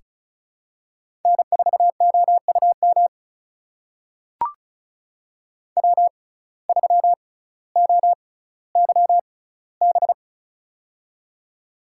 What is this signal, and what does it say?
Morse code, heard as a keyed tone, about 36 wpm, spaced as for 18 wpm, 700 Hz. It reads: N4OUM W3OYB
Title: USA Call Signs - 5 Characters - 6x Character Spacing - 35wpm